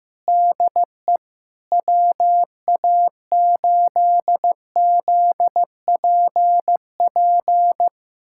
Morse code, heard as a keyed tone, about 15 wpm, 700 Hz. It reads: DE WA8ZPP
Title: USA Callsign DE Callsign - 6 Characters - 15wpm